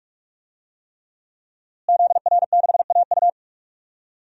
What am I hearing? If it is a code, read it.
Morse code, heard as a keyed tone, about 46 wpm, 700 Hz. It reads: ZR6AU
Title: DXCC-100 Call Signs - 5 Characters - Repeated 3x - 45wpm